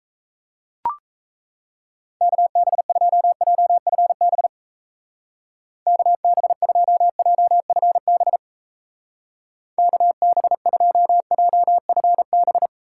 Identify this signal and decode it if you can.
Morse code, heard as a keyed tone, about 34 wpm, 700 Hz. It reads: KB2JFB KB2JFB KB2JFB
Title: Speed Racing - USA 2x3 Call Signs 28wpm